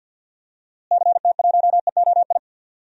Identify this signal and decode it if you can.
Morse code, heard as a keyed tone, about 47 wpm, 700 Hz. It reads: CT1EOI
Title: DXCC-100 Call Signs - 6 Characters - Repeated 3x - 50wpm